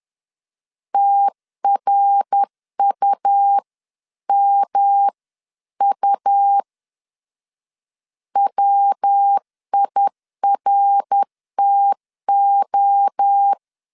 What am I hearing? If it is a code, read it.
Morse code, heard as a keyed tone, about 10 wpm, 800 Hz. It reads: TRUMU WIRTO
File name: CW / DET / 1 3 1500C S C 10wpm 3335